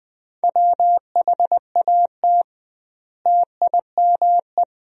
Morse code, heard as a keyed tone, about 21 wpm, 700 Hz. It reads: WHAT TIME
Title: 2x Word-Spacing - Sentences from Top 100 Words - 20wpm